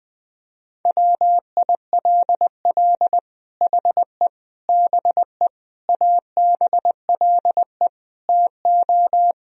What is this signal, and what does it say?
Morse code, heard as a keyed tone, about 20 wpm, 700 Hz. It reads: WILL HE BE ABLE TO
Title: Sentences from Top 1000 Words - Review Part 3 - 20wpm